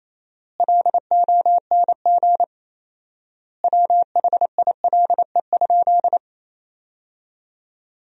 Morse code, heard as a keyed tone, about 27 wpm, 700 Hz. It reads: LONG WHILE?
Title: 4x Word-Spacing - Sentences from Top 300 Words - 28wpm